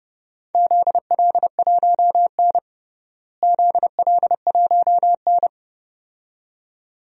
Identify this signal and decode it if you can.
Morse code, heard as a keyed tone, about 32 wpm, 700 Hz. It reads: ZL1N ZL1N
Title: DXCC-100 Call Signs - 4 Characters - Repeated 2x - 30wpm